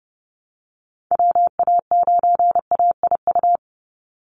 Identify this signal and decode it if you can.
Morse code, heard as a keyed tone, about 30 wpm, 700 Hz. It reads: WA9AIU
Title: USA Call Signs 30wpm